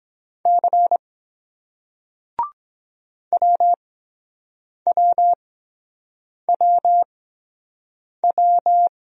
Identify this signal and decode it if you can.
Morse code, heard as a keyed tone, about 23 wpm, spaced as for 7 wpm, 700 Hz. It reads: C WWWW